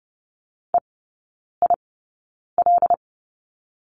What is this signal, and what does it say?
Morse code, heard as a keyed tone, about 30 wpm, spaced as for 9 wpm, 700 Hz. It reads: EIL
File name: Lesson 18e - Letter m - Three Characters 30wpm